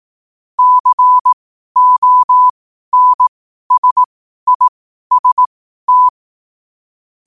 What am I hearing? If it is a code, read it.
Morse code, heard as a keyed tone, about 18 wpm, spaced as for 12 wpm, 1000 Hz. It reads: CONSIST
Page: New quotes every day in morse code at 12 Words per minute.